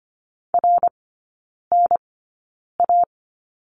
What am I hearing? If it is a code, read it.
Morse code, heard as a keyed tone, about 25 wpm, spaced as for 9 wpm, 690 Hz. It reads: RNA
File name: Lesson 14e - Number 2 - Three Characters 25wpm